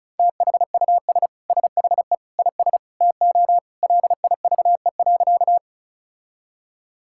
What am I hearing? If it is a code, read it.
Morse code, heard as a keyed tone, about 36 wpm, 700 Hz. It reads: THUS SHE IS TO LIVE.
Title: Sentences from Top 1000 Words - Review Part 5 - 35wpm